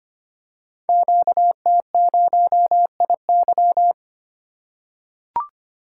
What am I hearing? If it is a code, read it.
Morse code, heard as a keyed tone, about 25 wpm, 700 Hz. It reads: QT0IY
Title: Code Groups - 5 Letter:Numbers - 25wpm